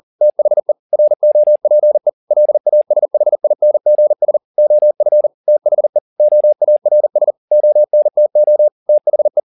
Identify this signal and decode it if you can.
Morse code, heard as a keyed tone, about 40 wpm, 600 Hz. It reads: THE ROPE LASHINGS OF THE OARS ONTO THE